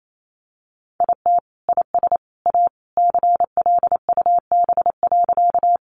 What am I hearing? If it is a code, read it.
Morse code, heard as a keyed tone, about 28 wpm, 700 Hz. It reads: IT IS A CLUB.
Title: Sentences from Top 1200 Words 28wpm